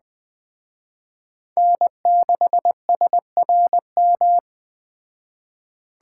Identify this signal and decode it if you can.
Morse code, heard as a keyed tone, about 20 wpm, 700 Hz. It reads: N6SRM